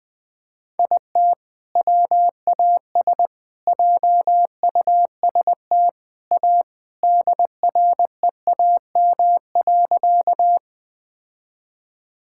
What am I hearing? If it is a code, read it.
Morse code, heard as a keyed tone, about 20 wpm, 700 Hz. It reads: IT WAS JUST A DREAM.
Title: Sentences with 3-Word Phrases - Spoken Beginning - 20wpm